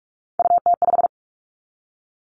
Morse code, heard as a keyed tone, about 46 wpm, 700 Hz. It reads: UT5
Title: Callsign Prefixes for Top 25 DXCC - 45wpm